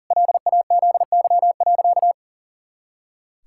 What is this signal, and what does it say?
Morse code, heard as a keyed tone, about 40 wpm, 700 Hz. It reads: LAZY.
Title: Pangram Challenge v2 - 40wpm